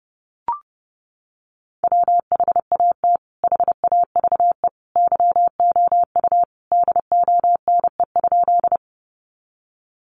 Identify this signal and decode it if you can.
Morse code, heard as a keyed tone, about 30 wpm, 700 Hz. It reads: WHAT HAVE YOU DONE?
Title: Sentences from Top 500 Words 30wpm